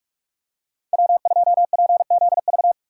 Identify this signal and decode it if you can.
Morse code, heard as a keyed tone, about 44 wpm, 700 Hz. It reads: W2PZV
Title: USA Call Signs - 5 Characters - 6x Character Spacing - 45wpm